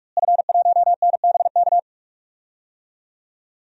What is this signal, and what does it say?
Morse code, heard as a keyed tone, about 41 wpm, 700 Hz. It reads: F1NBK
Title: DXCC-100 Call Signs - 5 Characters - Repeated 2x - 45wpm